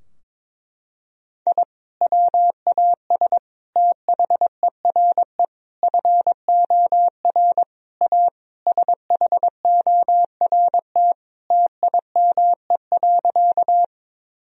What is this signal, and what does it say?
Morse code, heard as a keyed tone, about 22 wpm, 700 Hz. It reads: I WAS THERE FOR A SHORT TIME.
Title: Sentences from Top 1000 Words - Review Part 2 - 22wpm